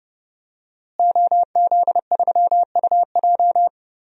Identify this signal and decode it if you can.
Morse code, heard as a keyed tone, about 31 wpm, 700 Hz. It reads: OZ3UJ
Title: DXCC-100 Call Signs - 5 Characters - Repeated 2x - 30wpm